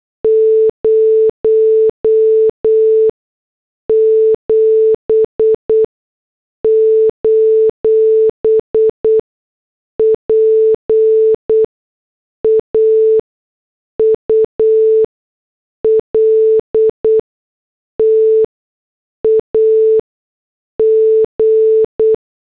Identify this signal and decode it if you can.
Morse code, heard as a keyed tone, about 8 wpm, spaced as for 6 wpm, 440 Hz. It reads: 07: PAUL TAG